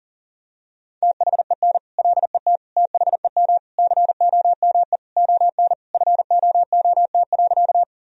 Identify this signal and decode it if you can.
Morse code, heard as a keyed tone, about 40 wpm, 700 Hz. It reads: THEN LET THEM COME ON FOOT.
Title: Sentences from Top 1000 Words - Review Part 3 - 40wpm